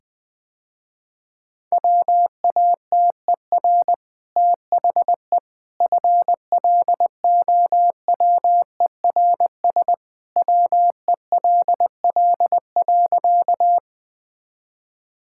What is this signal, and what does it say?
Morse code, heard as a keyed tone, about 21 wpm, 700 Hz. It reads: WATER THE FLOWERS WELL.
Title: Easy Sentences with Hints - 20wpm